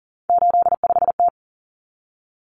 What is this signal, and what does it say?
Morse code, heard as a keyed tone, about 40 wpm, 700 Hz. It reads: 85T
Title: Common Cut Numbers - 3-Digit 40wpm